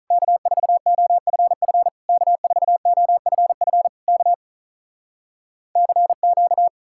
Morse code, heard as a keyed tone, about 48 wpm, 700 Hz. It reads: K4OFF K4OFF K CQ